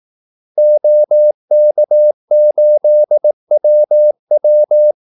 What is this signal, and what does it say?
Morse code, heard as a keyed tone, about 18 wpm, 600 Hz. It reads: OK8WW
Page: Kaldesignaler 18 wpm